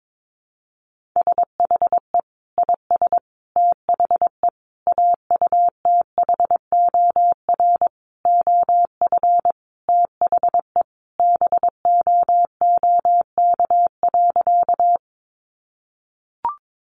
Morse code, heard as a keyed tone, about 22 wpm, 700 Hz. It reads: SHE IS THE AUTHOR OF THE BOOK.
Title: Sentences from Top 1200 Words 22wpm